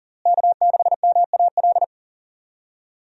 Morse code, heard as a keyed tone, about 40 wpm, 700 Hz. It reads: K6MAL
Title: USA Call Signs - 5 Characters - Repeated 3x - 40wpm